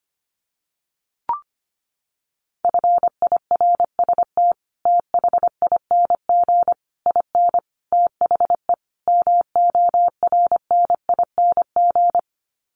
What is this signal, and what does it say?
Morse code, heard as a keyed tone, about 25 wpm, 700 Hz. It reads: FIRST THING IN THE MORNING
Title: Sets of 5 Words Encore2 25wpm